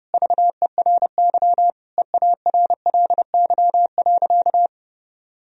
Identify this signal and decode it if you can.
Morse code, heard as a keyed tone, about 32 wpm, 700 Hz. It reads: VERY EARLY.
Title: Sentences from Top 1000 Words - Review Part 2 - 30wpm